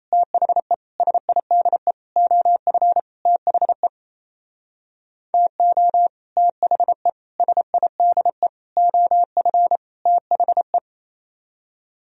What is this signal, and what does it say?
Morse code, heard as a keyed tone, about 30 wpm, 700 Hz. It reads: THE SIDE OF THE TO THE SIDE OF THE
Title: Speed-Racing - Sets of 5 Words 28wpm